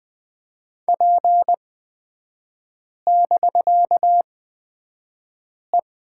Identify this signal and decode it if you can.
Morse code, heard as a keyed tone, about 20 wpm, spaced as for 5 wpm, 700 Hz. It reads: P<BK>E
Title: 255 Lesson 41e - Period - Three Characters 20wpm